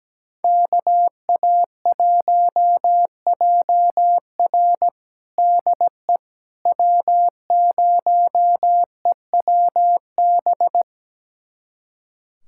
Morse code, heard as a keyed tone, about 17 wpm, 700 Hz. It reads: KA1JR DE W0EWB
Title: USA Callsign DE Callsign - 5 Characters - 17wpm